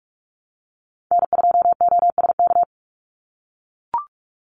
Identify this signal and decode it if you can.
Morse code, heard as a keyed tone, about 44 wpm, 700 Hz. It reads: N2OSK
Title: USA Call Signs 45wpm